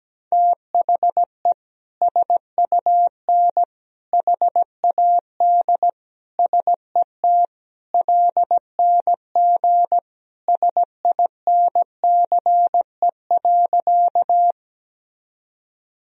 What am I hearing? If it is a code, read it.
Morse code, heard as a keyed tone, about 17 wpm, 700 Hz. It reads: THE SUN HAD SET LNG SINCE.
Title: Sentences from Top 300 Words and Abbreviations 17wpm